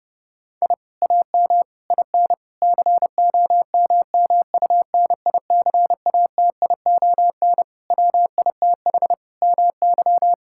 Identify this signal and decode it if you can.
Morse code, heard as a keyed tone, about 32 wpm, 700 Hz. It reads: I AM IN COMMUNICATION WITH MY
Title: Sentences from Top 1000 Words - Review Part 6 - 30wpm